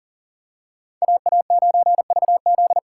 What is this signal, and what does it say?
Morse code, heard as a keyed tone, about 40 wpm, 700 Hz. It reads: AA9VZ
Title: USA Call Signs - 6x Character Spacing - 40wpm